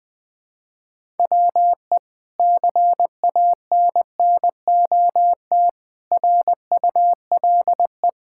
Morse code, heard as a keyed tone, about 20 wpm, 700 Hz. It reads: WE CANNOT RULE
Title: Sentences with 8-Word Phrases - 20wpm